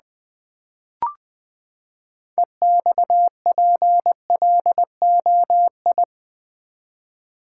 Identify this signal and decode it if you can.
Morse code, heard as a keyed tone, about 20 wpm, 700 Hz. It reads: EXPLOI
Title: Build A Word - 6 Letters At A Time - 12-Letter Words - 20wpm